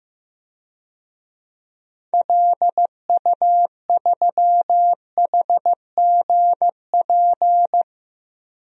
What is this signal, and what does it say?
Morse code, heard as a keyed tone, about 15 wpm, 700 Hz. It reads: LU3HGP
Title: DXCC-100 Call Signs - 6 Characters - 15wpm